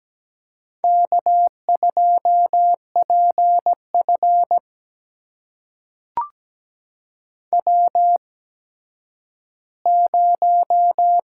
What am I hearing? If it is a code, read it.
Morse code, heard as a keyed tone, about 17 wpm, spaced as for 10 wpm, 700 Hz. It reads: K2PF W0
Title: USA Call Signs - 4 Characters - 8x Character Spacing - 17wpm